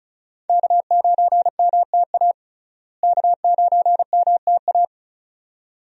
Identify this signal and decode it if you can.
Morse code, heard as a keyed tone, about 34 wpm, 700 Hz. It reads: K9MTA K9MTA
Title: USA Call Signs - 5 Characters - Repeated 2x - 35wpm